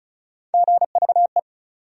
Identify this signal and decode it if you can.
Morse code, heard as a keyed tone, about 36 wpm, 700 Hz. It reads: GVE
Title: USA Call Sign Postfixes - 3 Letters 35wpm